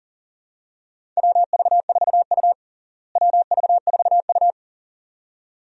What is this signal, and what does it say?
Morse code, heard as a keyed tone, about 40 wpm, 700 Hz. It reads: WV4U WV4U
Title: USA Call Signs - 4 Characters - Repeated 2x - 40wpm